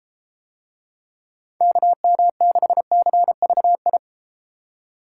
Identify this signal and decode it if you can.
Morse code, heard as a keyed tone, about 32 wpm, 700 Hz. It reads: KM6CVI